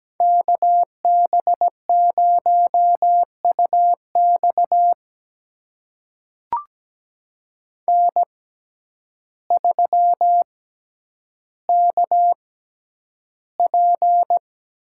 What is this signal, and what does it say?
Morse code, heard as a keyed tone, about 17 wpm, spaced as for 9 wpm, 700 Hz. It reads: KB0UX N3KP
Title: USA Call Signs - 5 Characters - 6x Character Spacing - 17wpm